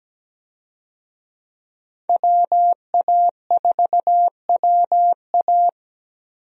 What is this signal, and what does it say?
Morse code, heard as a keyed tone, about 17 wpm, 700 Hz. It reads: WA4WA
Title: USA Call Signs - 5 Characters - 17wpm